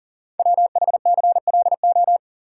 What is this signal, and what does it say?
Morse code, heard as a keyed tone, about 38 wpm, 690 Hz. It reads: WHCLO
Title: Code Groups - 5 Letter:Numbers - 40wpm